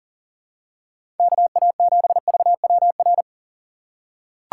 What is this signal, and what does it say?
Morse code, heard as a keyed tone, about 37 wpm, 690 Hz. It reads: KA7VWR
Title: USA Call Signs - 6x Character Spacing - 40wpm